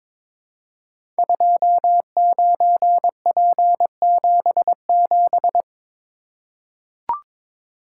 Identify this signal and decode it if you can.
Morse code, heard as a keyed tone, about 22 wpm, 700 Hz. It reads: 29P77
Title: Code Groups - 5 Letter:Numbers - 22wpm